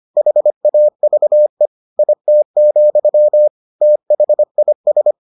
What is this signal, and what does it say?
Morse code, heard as a keyed tone, about 25 wpm, 600 Hz. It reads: HAVE IT, THIS